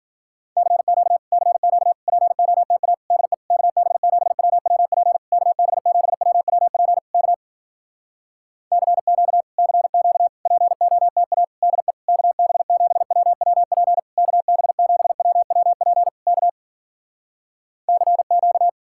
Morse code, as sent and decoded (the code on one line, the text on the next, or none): -.-. --.- / -.-. --.- / .--. --- - .- / -.. / -.- -... --... .-- .-- .--. / -.- -... --... .-- .-- .--. / -.- / -.-. --.- / -.-. --.- / .--. --- - .- / -.. / -.- -... --... .-- .-- .--. / -.- -... --... .-- .-- .--. / -.- / -.-. --.-
CQ CQ POTA DE KB7WWP KB7WWP K CQ CQ POTA DE KB7WWP KB7WWP K CQ